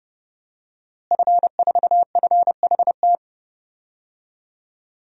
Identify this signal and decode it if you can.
Morse code, heard as a keyed tone, about 30 wpm, 700 Hz. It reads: F4FHT
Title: DXCC-100 Call Signs - 5 Characters - Repeated 2x - 30wpm